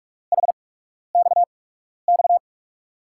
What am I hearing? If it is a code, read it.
Morse code, heard as a keyed tone, about 46 wpm, spaced as for 12 wpm, 700 Hz. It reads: HXX
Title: USA Call Signs - 6 Characters - 8x Character Spacing - 45wpm